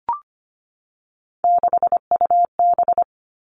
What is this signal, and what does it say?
Morse code, heard as a keyed tone, about 25 wpm, 700 Hz. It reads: 6UB
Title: Three Challenging Letter-Number Combinations 25wpm